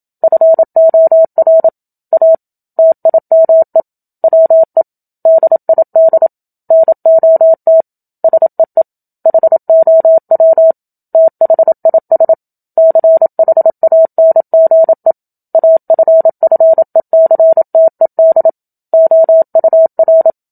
Morse code, heard as a keyed tone, about 27 wpm, 650 Hz. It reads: FOR A TIME WE DID NOT SEE HOW THIS CHANGE AFFECTED OUR
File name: Book Two / 01-Chapter 1 - 27 WPM